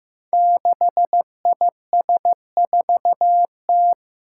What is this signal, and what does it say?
Morse code, heard as a keyed tone, about 15 wpm, 700 Hz. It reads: 6IS4T
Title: Five Challenging Letter-Number Combinations 15wpm